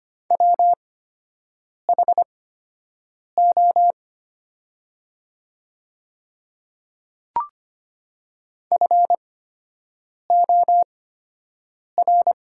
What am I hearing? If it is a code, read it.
Morse code, heard as a keyed tone, about 25 wpm, spaced as for 7 wpm, 700 Hz. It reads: WHO FOR
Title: 8x Character Spacing - Top 100 Words - 25wpm